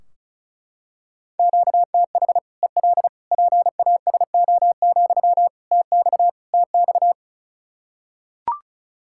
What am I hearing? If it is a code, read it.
Morse code, heard as a keyed tone, about 36 wpm, 700 Hz. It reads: QTH EL PASO, TX TX
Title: Practice QSO - Exchange 1 - Beginning - 35wpm